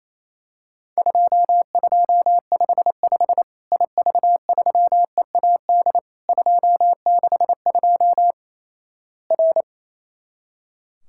Morse code, heard as a keyed tone, about 28 wpm, 700 Hz. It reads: 2255 IV3EAD 262 R
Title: Worked All Europe DX Contest - QTC Exchange - 28wpm